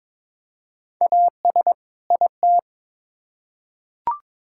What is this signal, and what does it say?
Morse code, heard as a keyed tone, about 22 wpm, 700 Hz. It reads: AS IT
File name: Build A Phrase - 2-Word Phrases - 22wpm